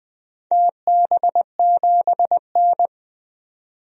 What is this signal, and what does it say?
Morse code, heard as a keyed tone, about 20 wpm, 700 Hz. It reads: TB7N
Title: DXCC-100 Call Signs - 4 Characters - 20wpm